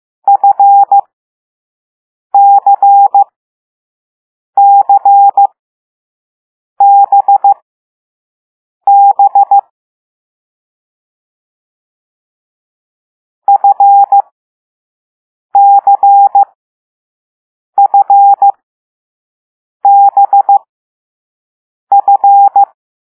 Morse code, heard as a keyed tone, about 15 wpm, spaced as for 5 wpm, 800 Hz. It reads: FCCBB FCFBF
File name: Morse / Level6B